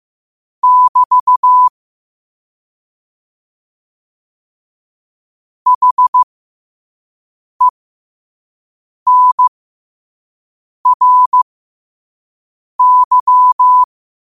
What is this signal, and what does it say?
Morse code, heard as a keyed tone, about 15 wpm, spaced as for 5 wpm, 1000 Hz. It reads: <BT> HENRY
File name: qotdfiles / 5wpm